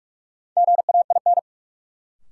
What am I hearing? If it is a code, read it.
Morse code, heard as a keyed tone, about 44 wpm, 700 Hz. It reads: GAIN
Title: Sentences from Top 1000 Words - New Word Drill - 45wpm